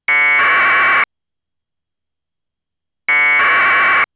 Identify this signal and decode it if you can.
Packet radio (AX.25, as in APRS).